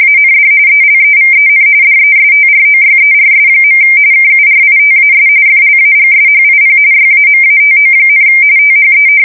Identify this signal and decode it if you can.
RTTY, about 45.45 baud. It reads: AND ELEVEN DESTROYERS. KURITAS FLAGSHIP YAMATO, THE
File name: fsk170 / battleshipnj